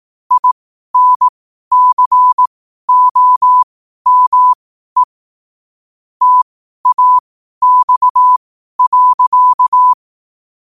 Morse code, heard as a keyed tone, about 18 wpm, spaced as for 12 wpm, 1000 Hz. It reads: INCOME TAX.
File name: qotdfiles / 12wpm